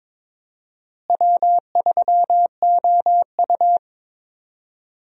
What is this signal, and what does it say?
Morse code, heard as a keyed tone, about 22 wpm, 700 Hz. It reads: W3OU
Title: USA Call Signs - 4 Characters - Repeated 2x - 22wpm